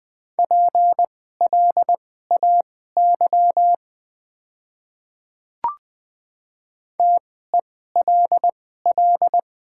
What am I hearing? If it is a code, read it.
Morse code, heard as a keyed tone, about 20 wpm, spaced as for 14 wpm, 700 Hz. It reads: PLAY TELL
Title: 2x Character Spacing - Top 500 Words - 20wpm